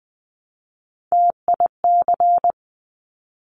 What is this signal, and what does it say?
Morse code, heard as a keyed tone, about 20 wpm, 700 Hz. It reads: TIC
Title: Top Three Letter Combinations in Words 20wpm